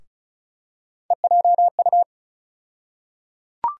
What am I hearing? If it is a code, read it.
Morse code, heard as a keyed tone, about 36 wpm, 700 Hz. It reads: EJU
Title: USA Call Sign Postfixes - 3 Letters 35wpm